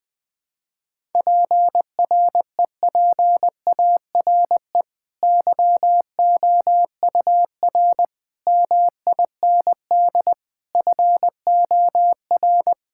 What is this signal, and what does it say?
Morse code, heard as a keyed tone, about 20 wpm, 700 Hz. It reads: PREPARE YOUR MIND FOR
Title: Sentences from Top 1000 Words - Review Part 6 - 20wpm